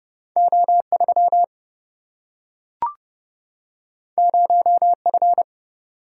Morse code, heard as a keyed tone, about 30 wpm, 700 Hz. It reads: O3 0F
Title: Single-Letter Single-Number 30wpm